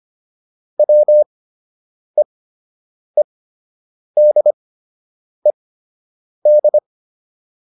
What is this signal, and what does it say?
Morse code, heard as a keyed tone, about 25 wpm, spaced as for 8 wpm, 600 Hz. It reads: WEEDED
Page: fundamental-practice-files